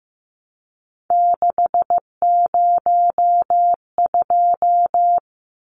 Morse code, heard as a keyed tone, about 15 wpm, 700 Hz. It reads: 602
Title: Numbers - 3-Digit 15wpm